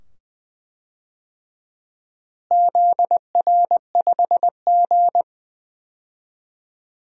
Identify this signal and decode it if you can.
Morse code, heard as a keyed tone, about 20 wpm, 700 Hz. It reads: ZR5G
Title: DXCC-100 Call Signs - 4 Characters - Repeated 2x - 20wpm